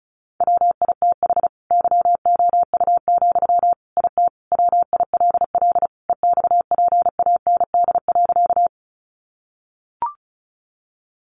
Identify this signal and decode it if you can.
Morse code, heard as a keyed tone, about 34 wpm, 700 Hz. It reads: WITH YOU, IT WILL EXPAND.
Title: Sentences from Top 1400 Words 35wpm